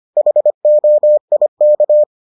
Morse code, heard as a keyed tone, about 25 wpm, 600 Hz. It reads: HOIK